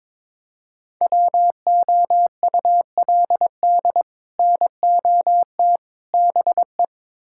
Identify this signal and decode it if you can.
Morse code, heard as a keyed tone, about 22 wpm, 700 Hz. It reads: WOULD NOT BE
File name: practice / 22wpm / Build A Phrase - 5-Word Phrases - 22wpm